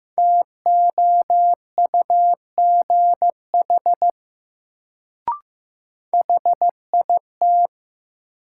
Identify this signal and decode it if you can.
Morse code, heard as a keyed tone, about 15 wpm, 700 Hz. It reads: TOUGH HIT
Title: Sentences from Top 1200 Words - New Word Drill Alt - 15wpm